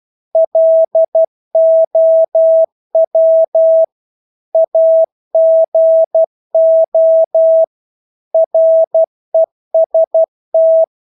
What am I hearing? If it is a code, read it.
Morse code, heard as a keyed tone, about 12 wpm, 650 Hz. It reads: LOW AGO REST
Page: Engelske ord 12 wpm